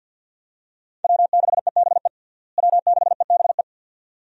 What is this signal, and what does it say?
Morse code, heard as a keyed tone, about 50 wpm, 700 Hz. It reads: W6EBE W6EBE
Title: USA Call Signs - 5 Characters - Repeated 2x - 50wpm